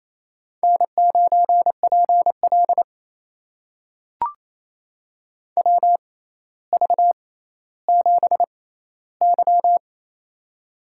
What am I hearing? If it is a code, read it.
Morse code, heard as a keyed tone, about 28 wpm, spaced as for 14 wpm, 700 Hz. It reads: N9PL WV7Y
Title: USA Call Signs - 4 Characters - 6x Character Spacing - 28wpm